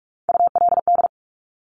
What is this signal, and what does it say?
Morse code, heard as a keyed tone, about 46 wpm, 700 Hz. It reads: ULD
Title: Top Three Letter Combinations in Words 45wpm